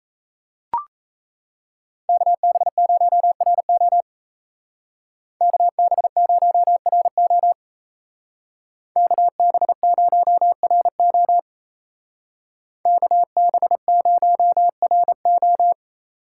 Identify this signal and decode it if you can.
Morse code, heard as a keyed tone, about 33 wpm, 700 Hz. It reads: KB0RO KB0RO KB0RO KB0RO
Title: Speed Racing - USA 2x2 Call Signs 28wpm